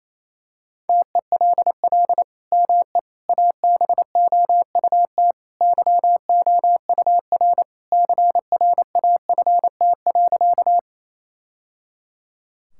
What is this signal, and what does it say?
Morse code, heard as a keyed tone, about 28 wpm, 700 Hz. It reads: TELL ME ABOUT YOUR CRAFT.
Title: Sentences from Top 1000 Words - Review Part 6 - 28wpm